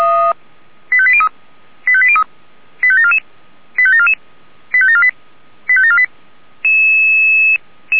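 Five-tone selective call (ZVEI), heard as quick runs of five tones, tones of about 70 ms. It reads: 85803 85803 85744 85744 85748 85748
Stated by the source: ZVEI five-tone selective call received on an SDR.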